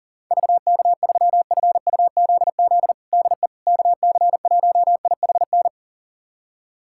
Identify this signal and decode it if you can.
Morse code, heard as a keyed tone, about 37 wpm, 700 Hz. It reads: VK3FUZZ DE KC1IHN
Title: USA Callsign DE Callsign 40wpm